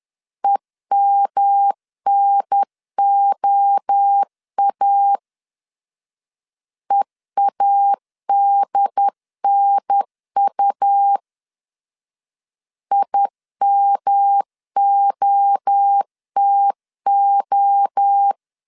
Morse code, heard as a keyed tone, about 10 wpm, 800 Hz. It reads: EMNOA EADNU IMOTO
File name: CW / DET / 1 3 1500C S C 10wpm 3335